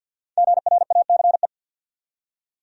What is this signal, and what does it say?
Morse code, heard as a keyed tone, about 50 wpm, 700 Hz. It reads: GRACE
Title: USA Common Names - Max 5 Letters - 50wpm